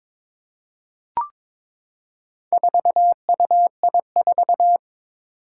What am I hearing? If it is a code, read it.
Morse code, heard as a keyed tone, about 21 wpm, 700 Hz. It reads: 4UI4
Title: Four Challenging Letter-Number Combinations 22wpm